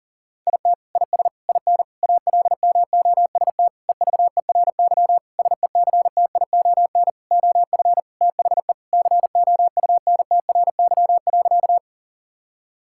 Morse code, heard as a keyed tone, about 40 wpm, 700 Hz. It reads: IT IS IN ALMOST EVERY SECTION OF THE COUNTRY.
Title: Sentences from Top 1000 Words - Review Part 5 - 40wpm